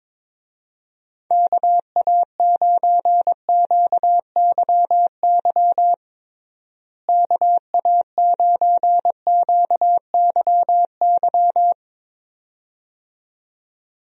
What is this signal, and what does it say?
Morse code, heard as a keyed tone, about 21 wpm, 700 Hz. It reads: KA9QYY KA9QYY
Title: USA Call Signs - 6 Characters - Repeated 2x - 22wpm